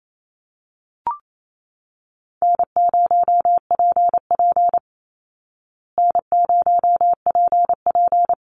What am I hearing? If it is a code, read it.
Morse code, heard as a keyed tone, about 28 wpm, 700 Hz. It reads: N0PP N0PP
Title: Call Signs Repeated 2x - 28wpm